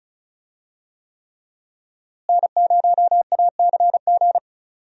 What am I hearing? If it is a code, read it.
Morse code, heard as a keyed tone, about 36 wpm, 700 Hz. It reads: N0ACG
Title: USA Call Signs - 5 Characters - Repeated 3x - 35wpm